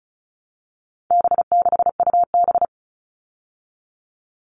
Morse code, heard as a keyed tone, about 34 wpm, 700 Hz. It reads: B6UB
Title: All Cut Numbers - 4-Digit 35wpm